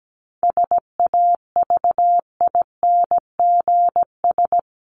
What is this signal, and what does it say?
Morse code, heard as a keyed tone, about 17 wpm, 700 Hz. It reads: SAVINGS
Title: Top 1001-1250 Words 17wpm